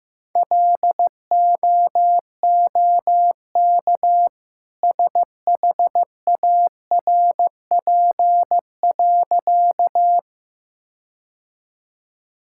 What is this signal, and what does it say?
Morse code, heard as a keyed tone, about 15 wpm, 700 Hz. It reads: LOOK SHARP.
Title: Sentences from Top 1600 Words 15wpm